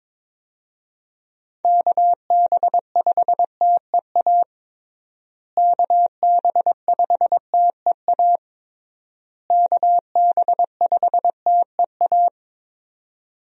Morse code, heard as a keyed tone, about 22 wpm, 700 Hz. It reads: KB5TEA KB5TEA KB5TEA
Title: USA Call Signs - 6 Characters - Repeated 3x - 22wpm